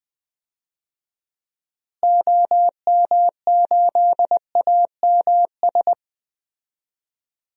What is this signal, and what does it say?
Morse code, heard as a keyed tone, about 20 wpm, 700 Hz. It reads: OM8AMS
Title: DXCC-100 Call Signs - 6 Characters - Repeated 3x - 20wpm